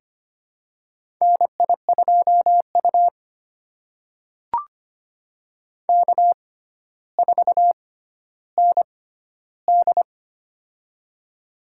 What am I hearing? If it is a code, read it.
Morse code, heard as a keyed tone, about 25 wpm, spaced as for 13 wpm, 700 Hz. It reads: NI2U K4ND
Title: USA Call Signs - 4 Characters - 6x Character Spacing - 25wpm